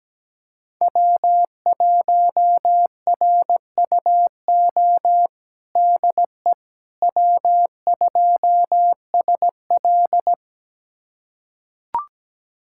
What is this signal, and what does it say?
Morse code, heard as a keyed tone, about 17 wpm, 700 Hz. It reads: W1RUO DE W2SL
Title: USA Callsign DE Callsign 17wpm